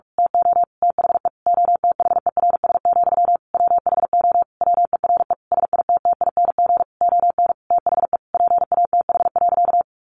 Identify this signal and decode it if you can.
Morse code, heard as a keyed tone, about 46 wpm, 700 Hz. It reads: TO THE OTHERS, WHO WERE SITTING ON THE PATH.
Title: A Sound Of Thunder 45wpm